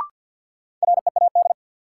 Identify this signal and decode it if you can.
Morse code, heard as a keyed tone, about 53 wpm, 700 Hz. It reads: READ
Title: Sentences from Top 200 Words - New Word Drill - 50wpm